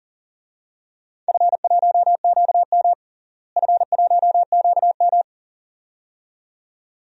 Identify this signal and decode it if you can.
Morse code, heard as a keyed tone, about 40 wpm, 700 Hz. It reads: F1QM F1QM
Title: DXCC-100 Call Signs - 4 Characters - Repeated 2x - 40wpm